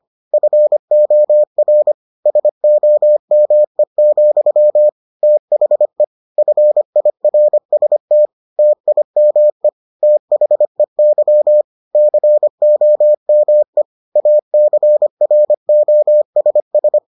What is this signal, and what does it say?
Morse code, heard as a keyed tone, about 25 wpm, 600 Hz. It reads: FOR SOME, THE FIRST TIME THEY COME ACROSS